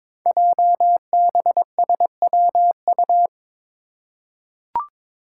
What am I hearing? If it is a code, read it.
Morse code, heard as a keyed tone, about 22 wpm, 700 Hz. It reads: JBSWU
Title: Code Groups - 5 Letter:Numbers - 22wpm